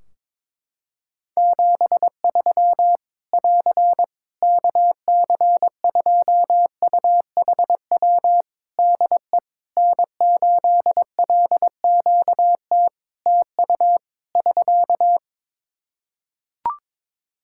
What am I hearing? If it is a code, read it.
Morse code, heard as a keyed tone, about 22 wpm, 700 Hz. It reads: 73 <AR> KC2UHW DE N8LQT TU <SK>
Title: Practice QSO - Exchange 4 - Ending - 22wpm